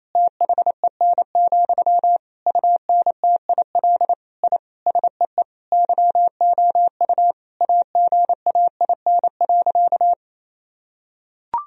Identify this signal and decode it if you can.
Morse code, heard as a keyed tone, about 28 wpm, 700 Hz. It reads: THEN, UNTIL I SEE YOU AGAIN.
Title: Sentences from Top 1000 Words - Review Part 1 - 28wpm